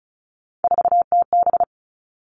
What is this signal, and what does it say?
Morse code, heard as a keyed tone, about 36 wpm, 700 Hz. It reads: 4TB
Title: Three Challenging Letter-Number Combinations 35wpm